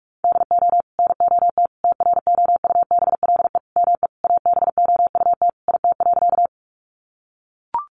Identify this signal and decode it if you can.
Morse code, heard as a keyed tone, about 44 wpm, 700 Hz. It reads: DO NOT TROUBLE ME ABOUT IT.
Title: Sentences from Top 1200 Words 45wpm